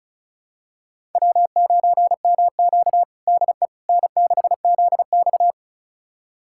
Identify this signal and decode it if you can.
Morse code, heard as a keyed tone, about 36 wpm, 700 Hz. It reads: W9MQ DE N6ZX
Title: USA Callsign DE Callsign - 4 Characters - 35wpm